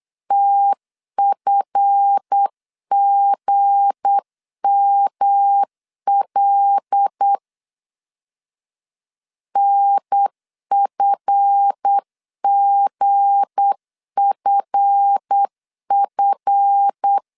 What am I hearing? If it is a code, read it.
Morse code, heard as a keyed tone, about 8 wpm, 800 Hz. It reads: TFGML NFGFF
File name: LEZIONI / 7 GF9Z / 7 GF 8wpm 3108